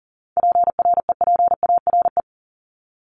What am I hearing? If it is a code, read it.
Morse code, heard as a keyed tone, about 40 wpm, 700 Hz. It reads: PREPARE
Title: ED Suffix - Top 100 Words with Base Words 40wpm